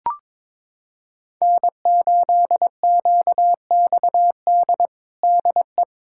Morse code, heard as a keyed tone, about 21 wpm, 700 Hz. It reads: N8QXD DE
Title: USA Callsign DE Callsign 22wpm